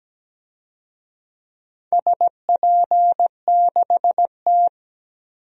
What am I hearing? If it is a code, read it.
Morse code, heard as a keyed tone, about 17 wpm, 700 Hz. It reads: SP6T
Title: DXCC-100 Call Signs - 4 Characters - 17wpm